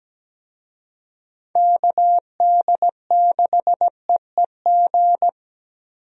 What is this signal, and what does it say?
Morse code, heard as a keyed tone, about 17 wpm, 700 Hz. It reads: KD6EEG
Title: USA Call Signs - 6 Characters - Repeated 2x - 17wpm